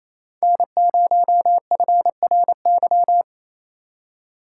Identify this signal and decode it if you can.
Morse code, heard as a keyed tone, about 32 wpm, 700 Hz. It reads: N0FRY
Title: Speed Racing - USA 1x3 Call Signs 28wpm